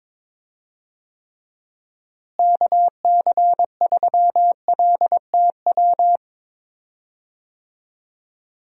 Morse code, heard as a keyed tone, about 22 wpm, 700 Hz. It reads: KC3LTW
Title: USA Call Signs - 6 Characters - Repeated 3x - 22wpm